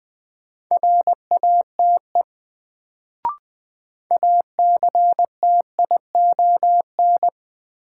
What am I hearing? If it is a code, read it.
Morse code, heard as a keyed tone, about 20 wpm, 700 Hz. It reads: RATE ACTION
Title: Sentences from Top 700 Words - New Word Drill - 20wpm